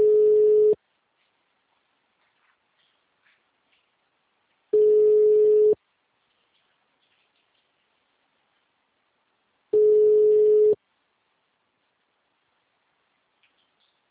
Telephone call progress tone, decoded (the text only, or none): ringback tone